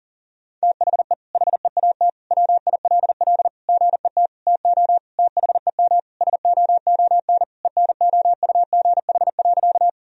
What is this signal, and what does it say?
Morse code, heard as a keyed tone, about 37 wpm, 700 Hz. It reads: THE HEAT WILL GET TO THEM SOON ENOUGH.
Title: Sentences from Top 1000 Words - Review Part 3 - 40wpm